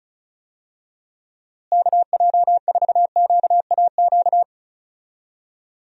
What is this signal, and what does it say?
Morse code, heard as a keyed tone, about 34 wpm, 700 Hz. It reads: KJ4QAQ
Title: USA Call Signs - 6 Characters - Repeated 2x - 35wpm